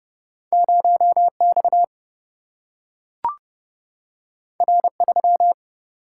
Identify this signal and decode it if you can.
Morse code, heard as a keyed tone, about 30 wpm, 700 Hz. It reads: 0 X R 3
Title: Single-Letter Single-Number 30wpm